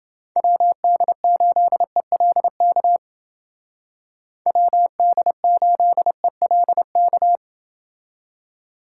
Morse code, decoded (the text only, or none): WD8ELK WD8ELK